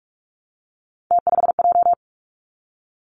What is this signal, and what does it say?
Morse code, heard as a keyed tone, about 46 wpm, 700 Hz. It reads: T5J
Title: Three Challenging Letter-Number Combinations 45wpm